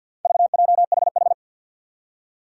Morse code, heard as a keyed tone, about 52 wpm, 700 Hz. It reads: VJHH
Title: Four Challenging Letter-Number Combinations 50wpm